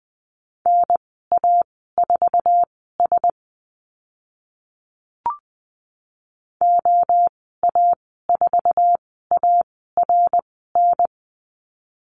Morse code, heard as a keyed tone, about 20 wpm, spaced as for 14 wpm, 700 Hz. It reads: NA4S OA4ARN
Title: 062 Lesson 11g - Letter h - Call Signs 20wpm